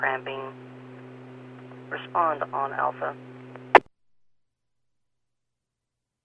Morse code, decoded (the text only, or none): none